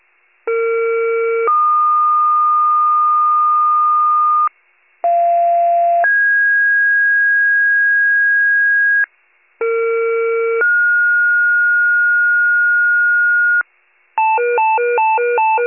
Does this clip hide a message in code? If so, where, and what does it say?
Yes: two-tone page from 0.5 to 13.6 s, heard as a one-second tone, then a three-second tone, about 460 then 1220 Hz, about 690 then 1650 Hz, about 460 then 1430 Hz, with noise.